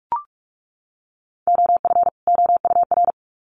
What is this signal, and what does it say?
Morse code, heard as a keyed tone, about 44 wpm, 700 Hz. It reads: OF OUR
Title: Sets of 2 Words 45wpm